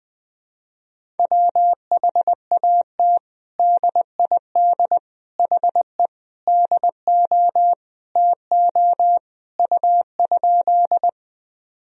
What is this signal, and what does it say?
Morse code, heard as a keyed tone, about 20 wpm, 700 Hz. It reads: WHAT DID HE DO TO U?
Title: Sentences from Top 100 Words and Abbreviations 20wpm